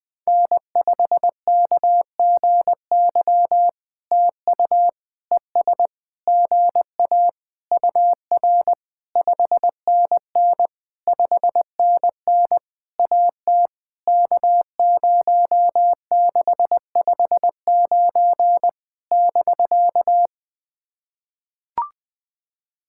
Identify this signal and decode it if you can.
Morse code, heard as a keyed tone, about 20 wpm, 700 Hz. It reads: N5KGY TU ES GA UR 5NN 5NN AT K0659 <BK>
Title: POTA QSO Practice - 20wpm